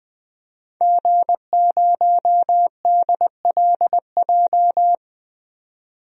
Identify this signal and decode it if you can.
Morse code, heard as a keyed tone, about 20 wpm, 700 Hz. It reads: G0DLJ
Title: DXCC-100 Call Signs - 5 Characters - 20wpm